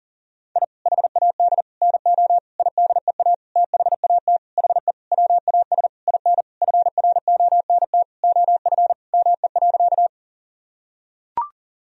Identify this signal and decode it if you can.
Morse code, heard as a keyed tone, about 40 wpm, 700 Hz. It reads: I HAD NO IDEA THAT HE WAS IN FRONT OF ME.
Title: Sentences from Top 1000 Words - Review Part 2 - 40wpm